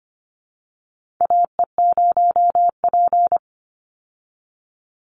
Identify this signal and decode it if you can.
Morse code, heard as a keyed tone, about 25 wpm, 700 Hz. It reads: AE0P
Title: Call Signs Repeated 3x - 25wpm